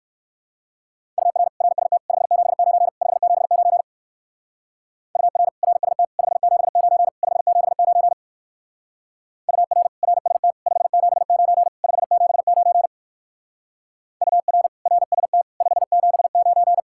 Morse code, heard as a keyed tone, about 55 wpm, 700 Hz. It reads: UR RST 579 579 UR RST 579 579 UR RST 579 579 UR RST 579
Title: Speed-Racing Common QSO Phrases Encore - 45wpm